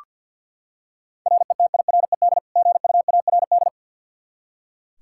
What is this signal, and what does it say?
Morse code, heard as a keyed tone, about 53 wpm, 700 Hz. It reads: RETIRED GUARD
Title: Practice QSO - Exchange 3 - More Info - 50wpm